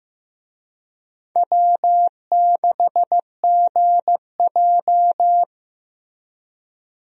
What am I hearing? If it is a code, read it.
Morse code, heard as a keyed tone, about 15 wpm, 700 Hz. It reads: W6GJ
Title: USA Call Signs - 4 Characters - Repeated 2x - 15wpm